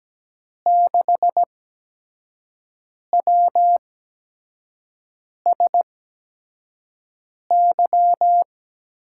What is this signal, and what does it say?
Morse code, heard as a keyed tone, about 17 wpm, spaced as for 5 wpm, 700 Hz. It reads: 6WSY
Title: USA Call Signs - 6 Characters - 8x Character Spacing - 17wpm